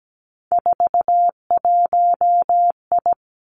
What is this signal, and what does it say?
Morse code, heard as a keyed tone, about 17 wpm, 700 Hz. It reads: 41I
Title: Three Challenging Letter-Number Combinations 17wpm